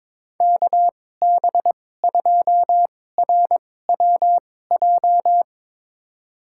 Morse code, heard as a keyed tone, about 21 wpm, spaced as for 16 wpm, 700 Hz. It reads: KB2RWJ
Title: USA Call Signs - 6 Characters - 2x Character Spacing - 22wpm